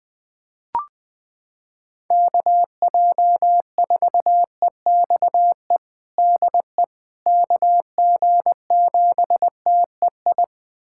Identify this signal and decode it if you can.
Morse code, heard as a keyed tone, about 21 wpm, 700 Hz. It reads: KJ4EXE DE KG7TEI
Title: USA Callsign DE Callsign - 6 Characters - 20wpm